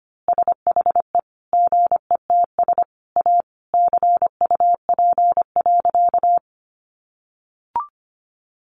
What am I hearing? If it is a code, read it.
Morse code, heard as a keyed tone, about 25 wpm, 700 Hz. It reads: SHE GETS A CUP.
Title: Easy Phrases 25wpm